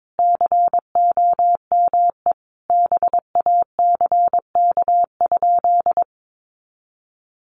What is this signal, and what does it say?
Morse code, heard as a keyed tone, about 22 wpm, 700 Hz. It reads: COME BACK?
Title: Sentences from Top 200 Words 22wpm